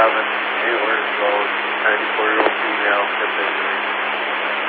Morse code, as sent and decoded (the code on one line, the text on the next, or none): none